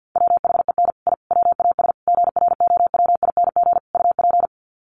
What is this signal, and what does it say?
Morse code, heard as a keyed tone, about 48 wpm, 700 Hz. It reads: WHEN I WAS GROWING UP
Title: Sets of 5 Words Encore 50wpm